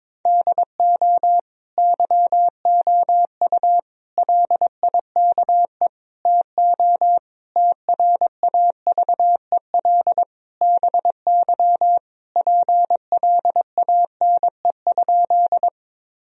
Morse code, heard as a keyed tone, about 22 wpm, 700 Hz. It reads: DO YOU LIKE TO TRAVEL BY PLANE?
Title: Sentences from Top 1000 Words - Review Part 3 - 22wpm